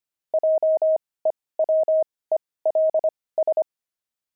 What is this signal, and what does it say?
Morse code, heard as a keyed tone, about 25 wpm, spaced as for 18 wpm, 630 Hz. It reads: JEWELS
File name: pangram-fs-18-cs-25-opal